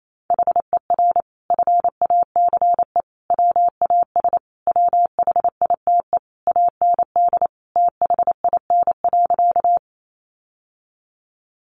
Morse code, heard as a keyed tone, about 28 wpm, 700 Hz. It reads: HER FACE WAS WHITE AND THIN.
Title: Sentences from Top 1200 Words 28wpm